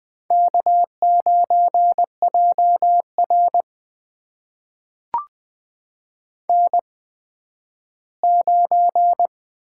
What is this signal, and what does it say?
Morse code, heard as a keyed tone, about 20 wpm, spaced as for 12 wpm, 700 Hz. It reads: K9JR N9
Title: USA Call Signs - 4 Characters - 8x Character Spacing - 20wpm